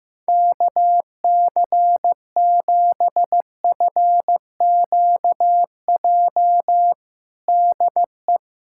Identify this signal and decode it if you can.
Morse code, heard as a keyed tone, about 16 wpm, 700 Hz. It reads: KC7FQJ DE
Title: USA Callsign DE Callsign - 6 Characters - 15wpm